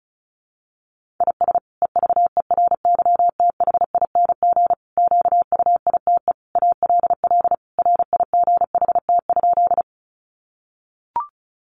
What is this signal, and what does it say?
Morse code, heard as a keyed tone, about 34 wpm, 700 Hz. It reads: IS EVERYTHING QUITE ALL RIGHT?
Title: Sentences from Top 700 Words 35wpm